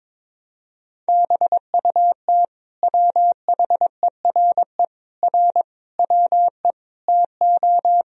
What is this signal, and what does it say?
Morse code, heard as a keyed tone, about 22 wpm, 700 Hz. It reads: BUT WHERE R WE TO
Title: Sentences from Top 200 Words and Abbreviations 22wpm